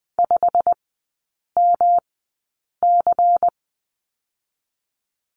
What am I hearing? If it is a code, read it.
Morse code, heard as a keyed tone, about 20 wpm, spaced as for 8 wpm, 700 Hz. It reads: 5MC
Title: 138 Lesson 23e - Letter f - Three Characters 20wpm